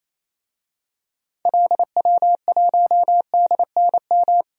Morse code, heard as a keyed tone, about 28 wpm, 700 Hz. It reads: LW1DNM
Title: DXCC-100 Call Signs - 6 Characters - 28wpm